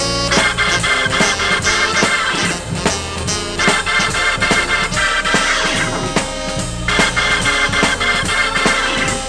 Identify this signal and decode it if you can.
Emergency Alert System (SAME) tones (header bursts), heard as harsh buzzing bursts. It is heard from 0.4 to 8.5 s.